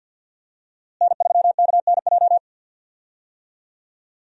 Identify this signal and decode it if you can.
Morse code, heard as a keyed tone, about 53 wpm, 700 Hz. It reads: N3KNJ
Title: USA Call Signs - 5 Characters - Repeated 2x - 50wpm